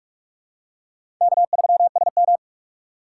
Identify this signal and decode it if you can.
Morse code, heard as a keyed tone, about 45 wpm, 700 Hz. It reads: K3SM
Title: USA Call Signs - 2x Character Spacing - 45wpm